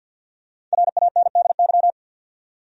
Morse code, heard as a keyed tone, about 53 wpm, 700 Hz. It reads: RANDY
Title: USA Common Names - 50wpm